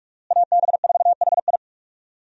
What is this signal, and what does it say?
Morse code, heard as a keyed tone, about 46 wpm, 700 Hz. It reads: AB4HI
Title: USA Call Signs - 5 Characters - Repeated 3x - 45wpm